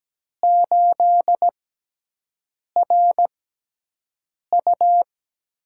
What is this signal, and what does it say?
Morse code, heard as a keyed tone, about 17 wpm, spaced as for 6 wpm, 700 Hz. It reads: 8RU
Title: USA Call Signs - 6x Character Spacing - 17wpm